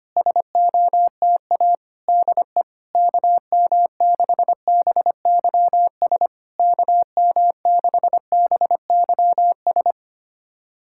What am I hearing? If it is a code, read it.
Morse code, heard as a keyed tone, about 25 wpm, 700 Hz. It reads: SOTA DE KM6BYS KM6BYS
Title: SOTA QSO Practice 25wpm